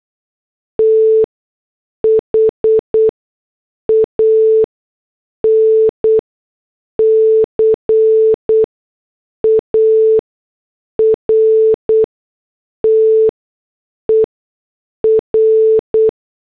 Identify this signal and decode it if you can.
Morse code, heard as a keyed tone, about 8 wpm, spaced as for 6 wpm, 440 Hz. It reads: THAN CARTER